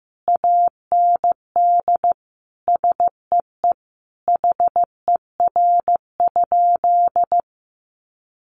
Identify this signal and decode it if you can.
Morse code, heard as a keyed tone, about 15 wpm, 700 Hz. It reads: AND SEE HER?
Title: Sentences from Top 100 Words 15wpm